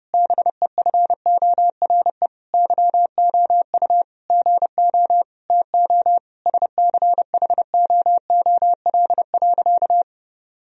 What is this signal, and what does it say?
Morse code, heard as a keyed tone, about 30 wpm, 700 Hz. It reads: BEFORE YOU GO TO SCHOOL.
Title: Sentences from Top 1000 Words - Review Part 2 - 30wpm